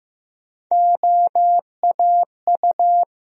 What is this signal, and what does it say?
Morse code, heard as a keyed tone, about 15 wpm, 700 Hz. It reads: OAU
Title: USA Call Sign Postfixes - 3 Letters 15wpm